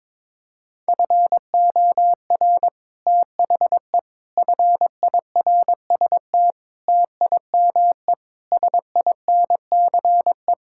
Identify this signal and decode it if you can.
Morse code, heard as a keyed tone, about 22 wpm, 700 Hz. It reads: FOR THE FIRST TIME SINCE
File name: practice / 22wpm / Build A Phrase - 5-Word Phrases - 22wpm